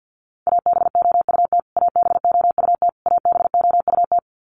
Morse code, heard as a keyed tone, about 52 wpm, 700 Hz. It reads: ABOUT ABOUT ABOUT
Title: Top 200 Words Repeated 50wpm